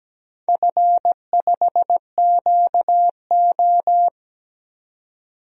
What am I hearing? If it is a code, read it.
Morse code, heard as a keyed tone, about 17 wpm, 700 Hz. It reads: F5QO
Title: DXCC-100 Call Signs - 4 Characters - 17wpm